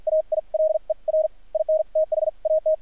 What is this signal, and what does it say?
Morse code, heard as a keyed tone, about 53 wpm, spaced as for 42 wpm, 590 Hz, with radio noise. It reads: AILER IN THAT
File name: archived / CRSnet